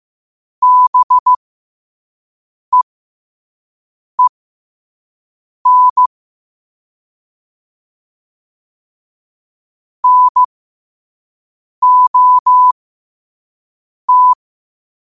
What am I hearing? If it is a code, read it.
Morse code, heard as a keyed tone, about 15 wpm, spaced as for 5 wpm, 1000 Hz. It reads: BEEN NOT
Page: Quotes for Wed, 13 Aug 2025 in Morse Code at 5 words per minute.